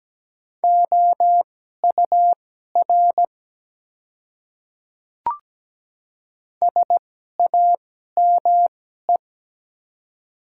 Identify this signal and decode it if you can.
Morse code, heard as a keyed tone, about 17 wpm, spaced as for 12 wpm, 700 Hz. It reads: OUR SAME
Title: 2x Character Spacing - Top 200 Words - 17wpm